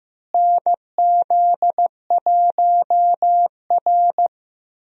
Morse code, heard as a keyed tone, about 16 wpm, 700 Hz. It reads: NZ1R
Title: USA Call Signs - 4 Characters - Repeated 3x - 15wpm